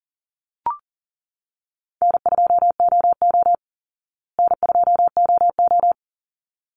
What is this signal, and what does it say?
Morse code, heard as a keyed tone, about 40 wpm, 700 Hz. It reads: N2OO N2OO
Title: Call Signs Repeated 3x - 40wpm